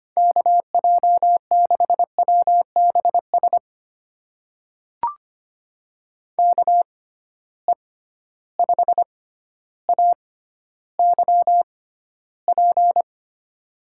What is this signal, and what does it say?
Morse code, heard as a keyed tone, about 25 wpm, spaced as for 13 wpm, 700 Hz. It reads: KJ6WBS KE5AYP
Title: USA Call Signs - 6 Characters - 6x Character Spacing - 25wpm